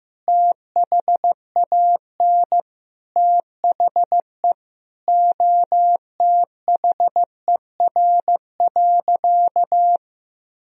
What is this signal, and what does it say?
Morse code, heard as a keyed tone, about 15 wpm, 700 Hz. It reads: THAN THE OTHER.
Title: Sentences from Top 1000 Words - Review Part 2 - 15wpm